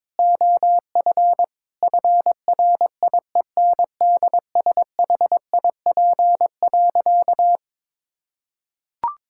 Morse code, heard as a keyed tone, about 21 wpm, 700 Hz. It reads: OF FRIENDSHIP.
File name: Difficult Sentences with Hints - 22wpm